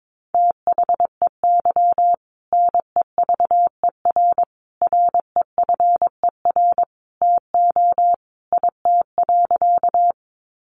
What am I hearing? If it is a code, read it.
Morse code, heard as a keyed tone, about 22 wpm, 700 Hz. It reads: THEY NEVER REFER TO IT.
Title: Sentences from Top 1000 Words 22wpm